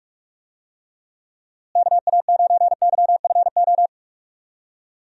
Morse code, heard as a keyed tone, about 44 wpm, 700 Hz. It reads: KA9YFO
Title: USA Call Signs - 6 Characters - Repeated 2x - 45wpm